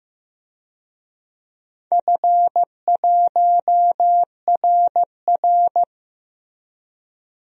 Morse code, heard as a keyed tone, about 15 wpm, 700 Hz. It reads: F1RR
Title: DXCC-100 Call Signs - 4 Characters - 15wpm